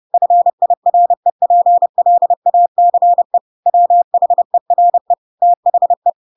Morse code, heard as a keyed tone, about 30 wpm, 700 Hz. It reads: FIREPLACE WHERE THE